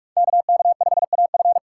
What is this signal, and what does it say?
Morse code, heard as a keyed tone, about 44 wpm, 700 Hz. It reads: KK5AF